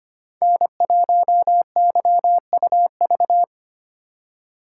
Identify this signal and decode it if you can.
Morse code, heard as a keyed tone, about 25 wpm, 700 Hz. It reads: N1YUV
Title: USA Call Signs - 4x Character Spacing - 25wpm